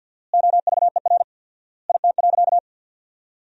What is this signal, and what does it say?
Morse code, heard as a keyed tone, about 50 wpm, 700 Hz. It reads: OVER IT.
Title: 4x Word-Spacing - Sentences from Top 100 Words - 50wpm